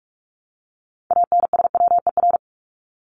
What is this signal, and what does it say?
Morse code, heard as a keyed tone, about 44 wpm, 700 Hz. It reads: ANSWER
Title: Top 300 Words 45wpm